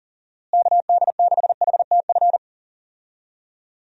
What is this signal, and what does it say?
Morse code, heard as a keyed tone, about 37 wpm, 700 Hz. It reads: KD6HTF
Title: USA Call Signs - 6 Characters - Repeated 2x - 40wpm